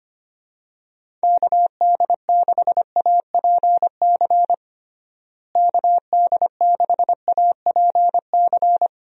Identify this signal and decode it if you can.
Morse code, heard as a keyed tone, about 25 wpm, 700 Hz. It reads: KD6APC KD6APC
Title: USA Call Signs - 6 Characters - Repeated 2x - 25wpm